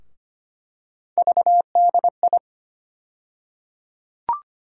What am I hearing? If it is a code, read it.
Morse code, heard as a keyed tone, about 25 wpm, 700 Hz. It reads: VDI
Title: USA Call Sign Postfixes - 3 Letters 25wpm